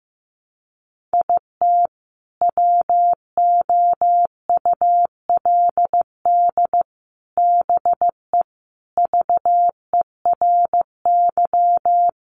Morse code, heard as a keyed tone, about 15 wpm, 700 Hz. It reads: IT WOULD BE VERY
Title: Sentences from Top 1100 Words 15wpm